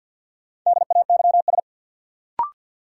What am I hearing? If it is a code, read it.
Morse code, heard as a keyed tone, about 52 wpm, 700 Hz. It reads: DAYS
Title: Sentences from Top 1000 Words - Review Part 5 - New Word Drill - 50wpm